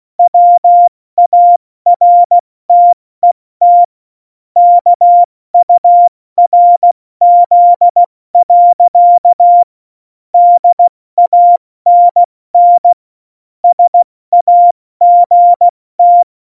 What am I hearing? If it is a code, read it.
Morse code, heard as a keyed tone, about 16 wpm, 700 Hz. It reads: WARTET KURZ. DANN SAGT
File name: Morsekurs / Witze / Witz 26. Januar 2021, 70 F80